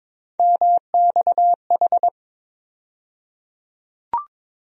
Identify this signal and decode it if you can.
Morse code, heard as a keyed tone, about 22 wpm, 700 Hz. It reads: MXH
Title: USA Call Sign Postfixes - 3 Letters 22wpm